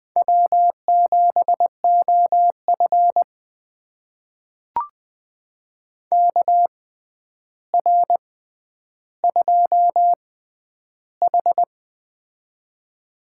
Morse code, heard as a keyed tone, about 20 wpm, spaced as for 10 wpm, 700 Hz. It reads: W7OF KR2H
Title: USA Call Signs - 4 Characters - 6x Character Spacing - 20wpm